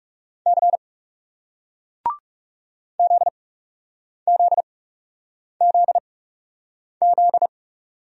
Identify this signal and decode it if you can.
Morse code, heard as a keyed tone, about 40 wpm, spaced as for 8 wpm, 700 Hz. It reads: C ZZZZ